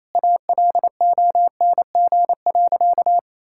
Morse code, heard as a keyed tone, about 28 wpm, 700 Hz. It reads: ALONG.
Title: Sentences from Top 1600 Words 28wpm